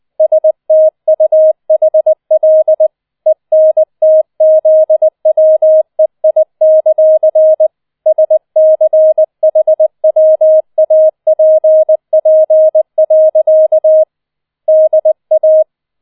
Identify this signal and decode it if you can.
Morse code, heard as a keyed tone, about 20 wpm, 630 Hz. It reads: STUHL ENTZWEI; SCHWAPP. DA